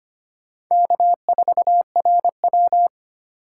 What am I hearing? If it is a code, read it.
Morse code, heard as a keyed tone, about 25 wpm, 700 Hz. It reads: K4RW
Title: USA Call Signs - 2x Character Spacing - 25wpm